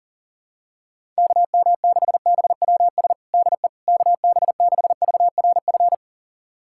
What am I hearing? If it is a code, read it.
Morse code, heard as a keyed tone, about 40 wpm, 700 Hz. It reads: KM6BWS DE KB6VRF
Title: USA Callsign DE Callsign - 6 Characters - 40wpm